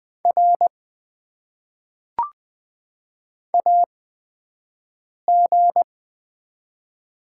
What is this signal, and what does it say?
Morse code, heard as a keyed tone, about 20 wpm, spaced as for 5 wpm, 700 Hz. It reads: R AG
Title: Top 90 QSO Elements - 8x Character Spacing - 20wpm